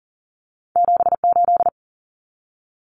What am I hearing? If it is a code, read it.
Morse code, heard as a keyed tone, about 40 wpm, 700 Hz. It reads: 78
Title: Common Cut Numbers - 2-Digit 40wpm